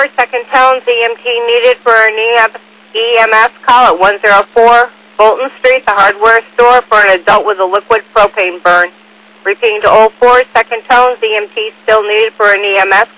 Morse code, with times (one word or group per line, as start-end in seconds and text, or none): none